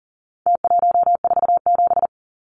Morse code, heard as a keyed tone, about 40 wpm, 700 Hz. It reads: T147
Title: Numbers - Contest Style - 4-Digit 40wpm